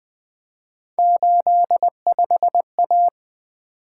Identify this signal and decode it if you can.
Morse code, heard as a keyed tone, about 20 wpm, 700 Hz. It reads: 85A
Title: Three Letter-Number Combinations - 20wpm